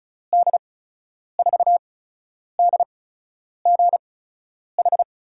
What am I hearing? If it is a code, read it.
Morse code, heard as a keyed tone, about 36 wpm, spaced as for 10 wpm, 700 Hz. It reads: D4DGH
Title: USA Call Signs - 6 Characters - 8x Character Spacing - 35wpm